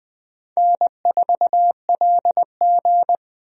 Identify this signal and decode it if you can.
Morse code, heard as a keyed tone, about 21 wpm, 700 Hz. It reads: N4LG
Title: USA Call Signs - 4 Characters - 20wpm